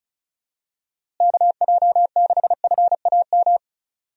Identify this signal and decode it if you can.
Morse code, heard as a keyed tone, about 34 wpm, 700 Hz. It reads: KJ6FAM
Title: USA Call Signs - 6 Characters - Repeated 2x - 35wpm